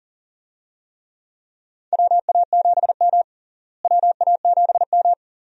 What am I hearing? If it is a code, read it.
Morse code, heard as a keyed tone, about 40 wpm, 700 Hz. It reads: WA7M WA7M
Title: USA Call Signs - 4 Characters - Repeated 2x - 40wpm